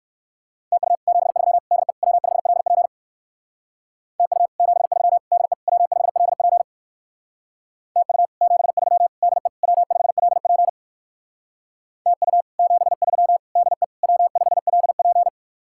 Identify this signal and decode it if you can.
Morse code, heard as a keyed tone, about 58 wpm, 700 Hz. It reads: TU 73 DE W5LP TU 73 DE W5LP TU 73 DE W5LP TU 73 DE W5LP